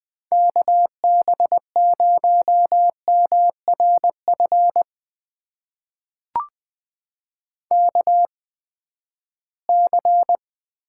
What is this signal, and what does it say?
Morse code, heard as a keyed tone, about 20 wpm, spaced as for 14 wpm, 700 Hz. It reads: KB0MRF KC
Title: USA Call Signs - 6 Characters - 8x Character Spacing - 20wpm